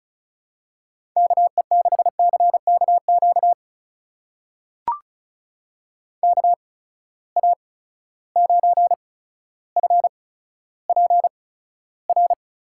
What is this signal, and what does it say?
Morse code, heard as a keyed tone, about 36 wpm, spaced as for 15 wpm, 700 Hz. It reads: KE6CKQ KA9FPR
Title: USA Call Signs - 6 Characters - 8x Character Spacing - 35wpm